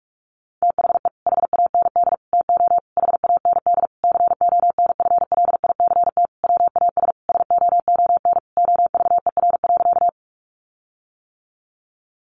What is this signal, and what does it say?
Morse code, heard as a keyed tone, about 44 wpm, 700 Hz. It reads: THE HAND TO HAND CONFLICT WAS SOON OVER.
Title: Sentences from Top 1300 Words 45wpm